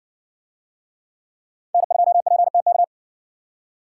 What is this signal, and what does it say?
Morse code, heard as a keyed tone, about 60 wpm, 700 Hz. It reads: N2PTK